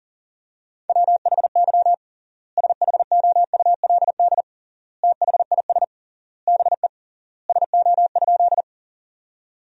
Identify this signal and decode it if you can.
Morse code, heard as a keyed tone, about 40 wpm, 700 Hz. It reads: WHY SHOULD THIS BE SO?
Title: 3x Word-Spacing - Sentences from Top 300 Words - 40wpm